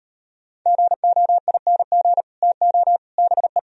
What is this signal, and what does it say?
Morse code, heard as a keyed tone, about 34 wpm, 700 Hz. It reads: GOING TO BE